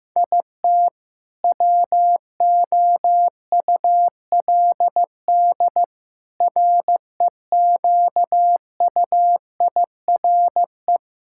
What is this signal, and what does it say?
Morse code, heard as a keyed tone, about 15 wpm, 700 Hz. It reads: IT WOULD REQUIRE
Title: Sentences from Top 1000 Words - Review Part 6 - 15wpm